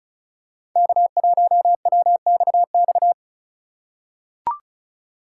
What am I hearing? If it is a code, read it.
Morse code, heard as a keyed tone, about 34 wpm, 700 Hz. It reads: K1WXX
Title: USA Call Signs - 5 Characters - 2x Character Spacing - 35wpm